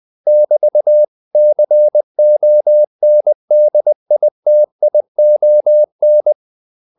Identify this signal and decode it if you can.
Morse code, heard as a keyed tone, about 20 wpm, 600 Hz. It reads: <BT>CONDITION
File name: CW / Practice / 500MostCommonEnglishWords20